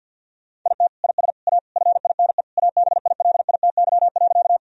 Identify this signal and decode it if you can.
Morse code, heard as a keyed tone, about 52 wpm, 700 Hz. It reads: IT IS A FINE ABILITY.
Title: Sentences from Top 1000 Words - Review Part 3 - 50wpm